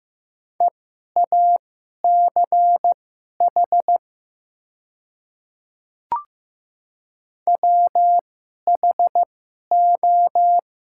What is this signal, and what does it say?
Morse code, heard as a keyed tone, about 15 wpm, spaced as for 11 wpm, 700 Hz. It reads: EACH WHO
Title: 2x Character Spacing - Top 100 Words - 15wpm